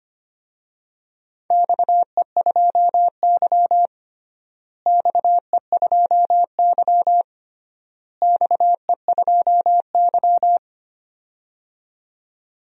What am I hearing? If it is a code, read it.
Morse code, heard as a keyed tone, about 25 wpm, 700 Hz. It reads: XE2Y XE2Y XE2Y
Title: DXCC-100 Call Signs - 4 Characters - Repeated 3x - 25wpm